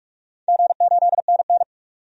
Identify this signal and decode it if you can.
Morse code, heard as a keyed tone, about 48 wpm, 700 Hz. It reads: G8NN
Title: DXCC-100 Call Signs - 4 Characters - 45wpm